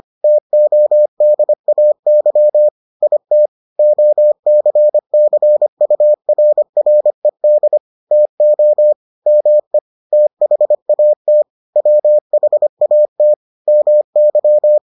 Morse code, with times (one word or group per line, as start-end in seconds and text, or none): 0.2-2.7 TODAY
3.0-3.5 IT
3.8-7.8 OCCURRED
8.1-8.9 TO
9.3-9.8 ME
10.1-11.4 THAT
11.8-13.3 WHAT
13.7-14.8 MY